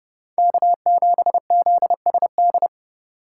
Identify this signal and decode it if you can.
Morse code, heard as a keyed tone, about 32 wpm, 700 Hz. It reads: K7ZSD
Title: USA Call Signs - 5 Characters - 2x Character Spacing - 30wpm